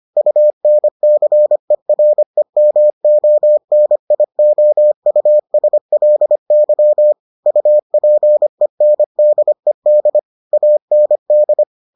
Morse code, heard as a keyed tone, about 25 wpm, 600 Hz. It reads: UNCEREMONIOUSLY UPENDED AND